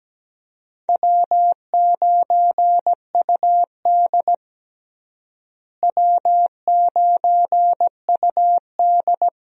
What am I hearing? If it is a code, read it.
Morse code, heard as a keyed tone, about 17 wpm, 700 Hz. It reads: W9UD W9UD
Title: USA Call Signs - 4 Characters - Repeated 3x - 17wpm